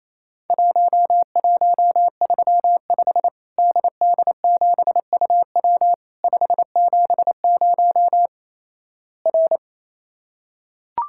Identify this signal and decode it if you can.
Morse code, heard as a keyed tone, about 28 wpm, 700 Hz. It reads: 1135 DD7UW 570 R
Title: Worked All Europe DX Contest - QTC Exchange - 28wpm